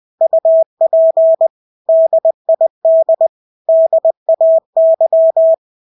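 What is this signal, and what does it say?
Morse code, heard as a keyed tone, about 20 wpm, 650 Hz. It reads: UP DID DAY
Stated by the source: Morse practice of English words at 20 wpm.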